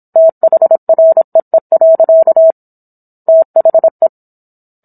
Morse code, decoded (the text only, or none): THREE. THE